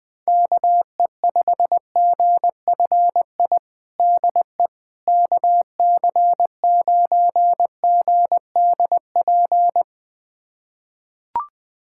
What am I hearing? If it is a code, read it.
Morse code, heard as a keyed tone, about 20 wpm, 700 Hz. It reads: KE5GFI DE KC9GDP
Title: USA Callsign DE Callsign 20wpm